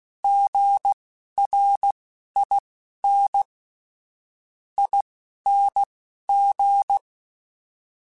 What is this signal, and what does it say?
Morse code, heard as a keyed tone, about 16 wpm, spaced as for 11 wpm, 800 Hz. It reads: GRIN ING
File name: Alphabetical / Alphabetical mp3s / Gs @10